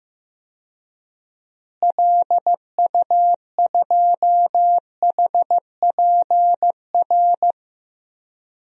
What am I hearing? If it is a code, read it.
Morse code, heard as a keyed tone, about 15 wpm, 700 Hz. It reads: LU2HPR
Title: DXCC-100 Call Signs - 6 Characters - Repeated 2x - 15wpm